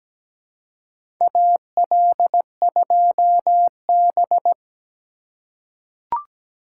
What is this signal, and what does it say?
Morse code, heard as a keyed tone, about 17 wpm, 700 Hz. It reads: AL2B
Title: USA Call Signs - 4 Characters - 3x Character Spacing - 17wpm